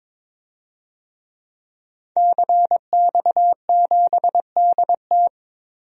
Morse code, heard as a keyed tone, about 22 wpm, 700 Hz. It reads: CX7DT
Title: DXCC-100 Call Signs - 5 Characters - 22wpm